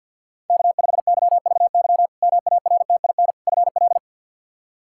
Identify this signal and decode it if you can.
Morse code, heard as a keyed tone, about 52 wpm, 700 Hz. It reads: K5YVY MARTIN FL
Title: K1USN SST QSO Practice - 50wpm